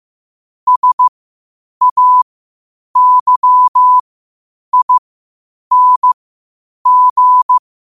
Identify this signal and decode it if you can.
Morse code, heard as a keyed tone, about 15 wpm, spaced as for 8 wpm, 1000 Hz. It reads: SAYING
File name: qotdfiles / 8wpm